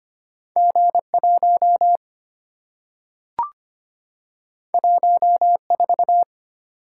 Morse code, heard as a keyed tone, about 25 wpm, 700 Hz. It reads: G1 14
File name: Two Letter-Number Combinations - 25wpm